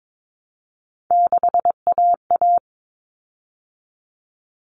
Morse code, heard as a keyed tone, about 22 wpm, 700 Hz. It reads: 6AA
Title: Common Cut Numbers - 3-Digit 22wpm